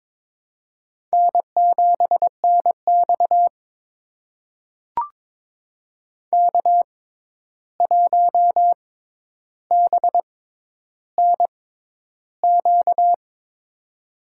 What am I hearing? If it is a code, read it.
Morse code, heard as a keyed tone, about 22 wpm, spaced as for 11 wpm, 700 Hz. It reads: N7NX K1BNQ
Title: USA Call Signs - 6x Character Spacing - 22wpm